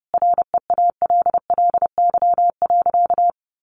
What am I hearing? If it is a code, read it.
Morse code, heard as a keyed tone, about 30 wpm, 700 Hz. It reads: REALLY.
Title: Sentences from Top 600 Words 30wpm